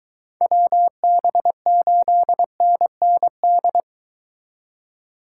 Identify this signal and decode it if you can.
Morse code, heard as a keyed tone, about 23 wpm, 700 Hz. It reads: WB8NND